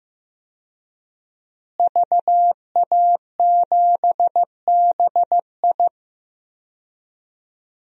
Morse code, heard as a keyed tone, about 16 wpm, 700 Hz. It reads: VA7BI
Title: DXCC-100 Call Signs - 5 Characters - Repeated 2x - 15wpm